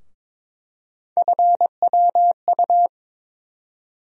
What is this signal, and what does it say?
Morse code, heard as a keyed tone, about 21 wpm, 700 Hz. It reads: FWU
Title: USA Call Sign Postfixes - 3 Letters 22wpm